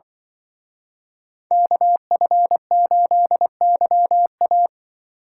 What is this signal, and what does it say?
Morse code, heard as a keyed tone, about 24 wpm, 700 Hz. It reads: KF8YA
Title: Speed Racing - USA Call Signs 20wpm